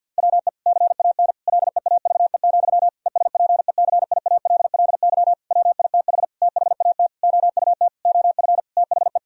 Morse code, heard as a keyed tone, about 50 wpm, 700 Hz. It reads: WE CAN LEAVE, ESPECIALLY WITH THAT OUT OF THE
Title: Sentences from Top 1000 Words - Review Part 4 - 50wpm